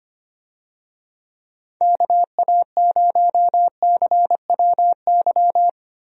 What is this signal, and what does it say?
Morse code, heard as a keyed tone, about 25 wpm, 700 Hz. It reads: KA0CWY
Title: USA Call Signs - 6 Characters - Repeated 3x - 25wpm